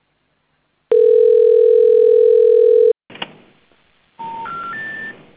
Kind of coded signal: telephone call progress tone